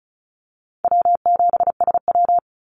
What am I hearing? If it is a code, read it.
Morse code, heard as a keyed tone, about 34 wpm, 700 Hz. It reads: W7SW
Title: Easy Call Signs Repeated 2x - 35wpm